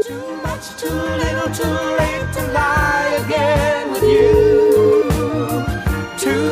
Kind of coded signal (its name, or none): telephone call progress tone